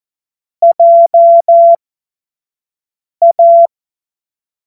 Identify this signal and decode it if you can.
Morse code, heard as a keyed tone, about 14 wpm, spaced as for 5 wpm, 680 Hz. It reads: JA
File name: Morsekurs / Witze / Witz 21. Februar 2023 25 F70